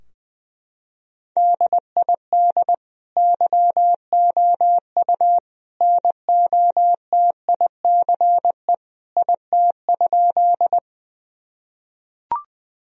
Morse code, heard as a keyed tone, about 20 wpm, 700 Hz. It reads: DID YOU NOTICE IT?
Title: Sentences from Top 1000 Words - Review Part 2 - 20wpm